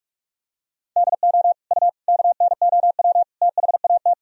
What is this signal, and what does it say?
Morse code, heard as a keyed tone, about 42 wpm, 700 Hz. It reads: DO U KNOW THAT
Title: Sentences from Top 400 Words and Abbreviations 45wpm